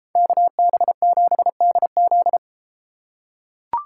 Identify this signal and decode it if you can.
Morse code, heard as a keyed tone, about 32 wpm, 700 Hz. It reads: KB7DZ